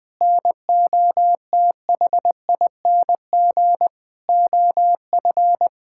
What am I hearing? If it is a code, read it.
Morse code, heard as a keyed tone, about 21 wpm, 700 Hz. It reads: NOTHING OF
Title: Sentences from Top 1600 Words 20wpm